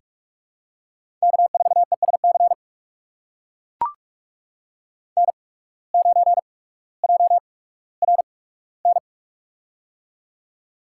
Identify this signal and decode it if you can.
Morse code, heard as a keyed tone, about 44 wpm, spaced as for 19 wpm, 700 Hz. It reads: K4ESC N9JRN
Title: USA Call Signs - 5 Characters - 8x Character Spacing - 45wpm